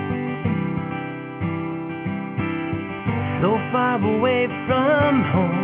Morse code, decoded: none